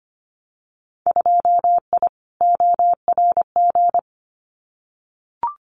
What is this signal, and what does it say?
Morse code, heard as a keyed tone, about 25 wpm, 700 Hz. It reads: 2I ORG
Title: Winter Field Day Exchanges Encore 25wpm